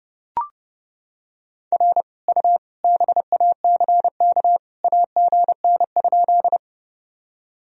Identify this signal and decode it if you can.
Morse code, heard as a keyed tone, about 30 wpm, 700 Hz. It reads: R U BACK AGN?
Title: Sentences from Top 200 Words and Abbreviations 30wpm